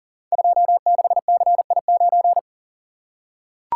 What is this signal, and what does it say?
Morse code, heard as a keyed tone, about 38 wpm, 700 Hz. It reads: 26CI9
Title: Code Groups - 5 Letter:Numbers - 40wpm